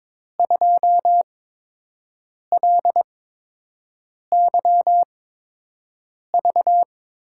Morse code, heard as a keyed tone, about 22 wpm, spaced as for 6 wpm, 700 Hz. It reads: 2LYV
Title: USA Call Signs - 6 Characters - 8x Character Spacing - 22wpm